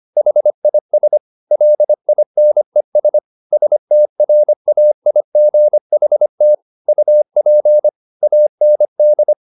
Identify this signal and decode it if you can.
Morse code, heard as a keyed tone, about 25 wpm, 600 Hz. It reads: HIS LINES STRAIGHT UP AND
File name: the old man and the sea part 1 25wpm